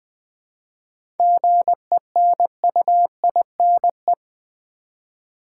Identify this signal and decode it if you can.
Morse code, heard as a keyed tone, about 20 wpm, 700 Hz. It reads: GENUINE
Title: 7 Letter Words - First 3 Letters Given - 20wpm